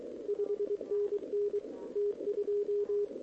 Morse code, heard as a keyed tone, about 23 wpm, 410 Hz. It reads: 5NN T2
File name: OK QRP 2026 / OKQRP2026